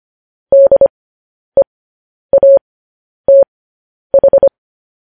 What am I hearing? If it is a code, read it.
Morse code, heard as a keyed tone, about 25 wpm, spaced as for 10 wpm, 550 Hz. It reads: DEATH
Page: news headlines in morse code at 10 wpm